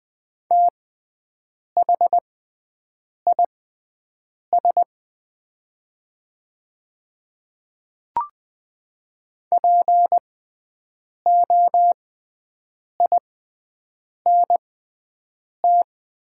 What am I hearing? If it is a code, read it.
Morse code, heard as a keyed tone, about 20 wpm, spaced as for 7 wpm, 700 Hz. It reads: THIS POINT
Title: 6x Character Spacing - Top 200 Words - 20wpm